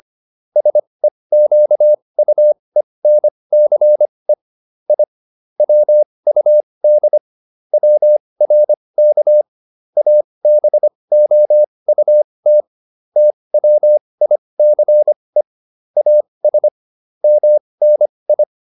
Morse code, heard as a keyed tone, about 25 wpm, spaced as for 20 wpm, 600 Hz. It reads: SEQUENCE I WUD WRK ABOUT TWICE AS MNI